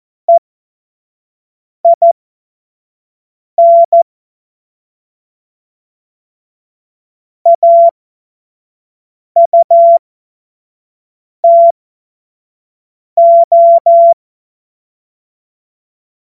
Morse code, heard as a keyed tone, about 14 wpm, spaced as for 5 wpm, 680 Hz. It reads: EIN AUTO
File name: Morsekurs / Witze / Witz 07. Dezember 2021, 25 F70